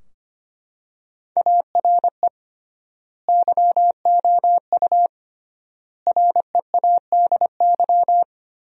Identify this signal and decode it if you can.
Morse code, heard as a keyed tone, about 25 wpm, 700 Hz. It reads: ARE YOU READY
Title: 3x Word-Spacing - Sentences from Top 400 Words - 25wpm